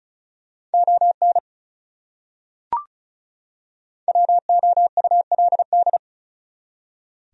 Morse code, heard as a keyed tone, about 34 wpm, 700 Hz. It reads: ON WOULD
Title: Top 100 Words 35wpm